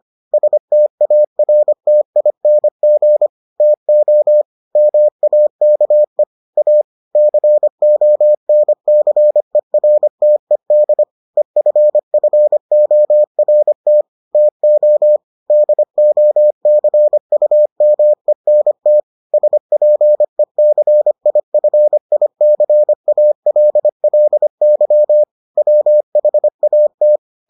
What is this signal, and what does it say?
Morse code, heard as a keyed tone, about 25 wpm, 600 Hz. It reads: STARTING TO MAKE A CONCERTED EFFORT TO DOCUMENT SPECIFICALLY WHAT